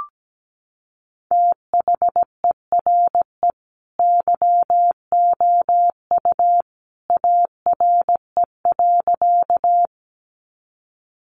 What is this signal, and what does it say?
Morse code, heard as a keyed tone, about 17 wpm, 700 Hz. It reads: THERE YOU ARE.
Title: Sentences from Top 100 Words 17wpm